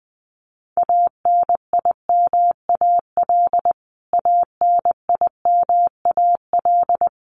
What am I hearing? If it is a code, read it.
Morse code, heard as a keyed tone, about 20 wpm, 700 Hz. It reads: ANIMAL ANIMAL
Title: Top 500 Words Repeated 20wpm